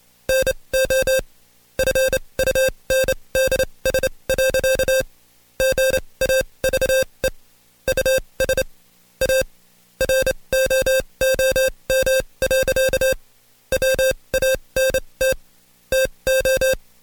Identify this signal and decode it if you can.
Morse code, heard as a keyed tone, about 27 wpm, spaced as for 22 wpm, 530 Hz. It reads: NO FUNDS. GAVE US A ROOM. WANT TO
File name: Code Quick / AudioDownld2 / Seg137-Stepper10 21WPM